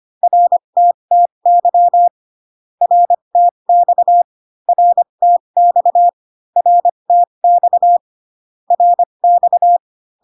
Morse code, heard as a keyed tone, about 25 wpm, spaced as for 22 wpm, 700 Hz. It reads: RTTY RTX RTX RTX RX
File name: CW common abbreviations alphabetical repeated 3 times 25 18wpm